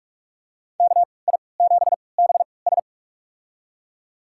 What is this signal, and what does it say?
Morse code, heard as a keyed tone, about 45 wpm, spaced as for 25 wpm, 700 Hz. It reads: KI7BS
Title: USA Call Signs - 5 Characters - 3x Character Spacing - 45wpm